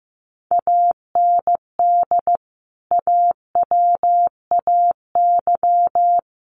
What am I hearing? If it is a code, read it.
Morse code, heard as a keyed tone, about 15 wpm, 700 Hz. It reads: AND AWAY
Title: Sentences from Top 600 Words 15wpm